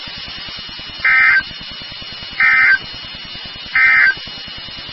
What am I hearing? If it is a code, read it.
Emergency Alert System (SAME) tones, heard as harsh buzzing bursts. It is heard from 0.9 to 4.2 s.
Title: NOAA Weather Radio WXK36